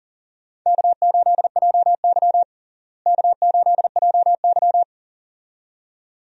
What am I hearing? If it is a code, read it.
Morse code, heard as a keyed tone, about 39 wpm, 700 Hz. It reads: K8JY K8JY
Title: USA Call Signs - 4 Characters - Repeated 2x - 40wpm